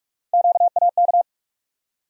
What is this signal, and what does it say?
Morse code, heard as a keyed tone, about 42 wpm, 700 Hz. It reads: QAK
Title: USA Call Sign Postfixes - 3 Letters 45wpm